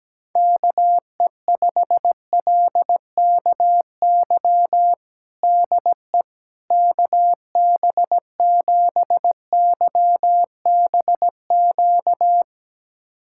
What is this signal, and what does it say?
Morse code, heard as a keyed tone, about 17 wpm, 700 Hz. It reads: KE5LKY DE KB7YBQ
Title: USA Callsign DE Callsign - 6 Characters - 17wpm